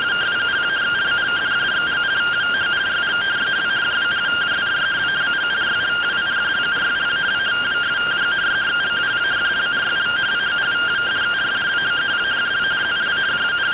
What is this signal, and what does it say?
Two-tone teleprinter signal, about 170 Hz shift.